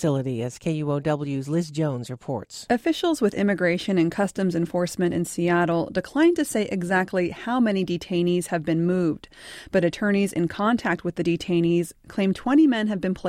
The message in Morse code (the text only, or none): none